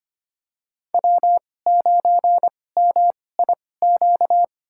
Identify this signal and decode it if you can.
Morse code, heard as a keyed tone, about 25 wpm, spaced as for 18 wpm, 700 Hz. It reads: W9MIQ
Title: USA Call Signs - 5 Characters - 2x Character Spacing - 25wpm